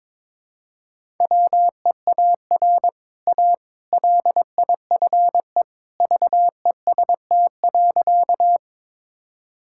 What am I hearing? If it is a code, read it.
Morse code, heard as a keyed tone, about 22 wpm, 700 Hz. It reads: WEAR A LIFE VEST.
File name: practice / 22wpm / Easy Sentences with Hints - 22wpm